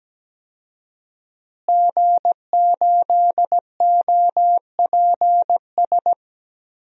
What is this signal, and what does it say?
Morse code, heard as a keyed tone, about 17 wpm, 700 Hz. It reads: G8OPS
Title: DXCC-100 Call Signs - 5 Characters - Repeated 2x - 17wpm